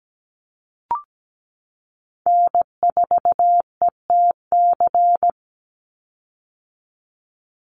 Morse code, heard as a keyed tone, about 17 wpm, 700 Hz. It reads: N4ETC
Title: Call Signs - 17wpm